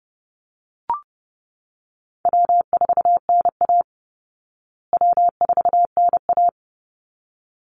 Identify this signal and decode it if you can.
Morse code, heard as a keyed tone, about 30 wpm, 700 Hz. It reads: W4NA W4NA
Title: Call Signs Repeated 3x - 30wpm